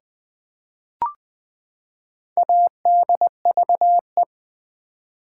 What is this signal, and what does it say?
Morse code, heard as a keyed tone, about 20 wpm, 700 Hz. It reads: ADVE
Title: Build A Word - 4 Letters At A Time - 11-Letter Words - 20wpm